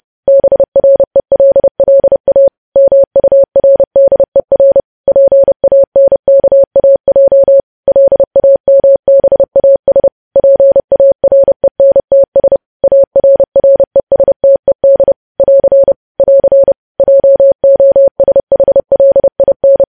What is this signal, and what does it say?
Morse code, heard as a keyed tone, about 30 wpm, 550 Hz. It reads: BRELLA MURDER PANKAJ LAMBAS PARENTS ARRESTED <AR> <AR> JOSHLIN
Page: news headlines in morse code at 30 wpm